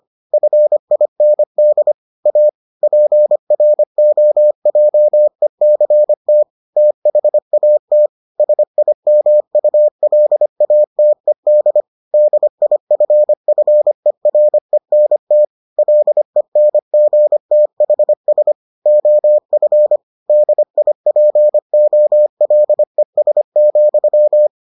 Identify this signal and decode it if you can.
Morse code, heard as a keyed tone, about 25 wpm, 600 Hz. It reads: FIND A PROJECT THAT SIMULATED DIFFERENT LENGTHS OF DIPOLES,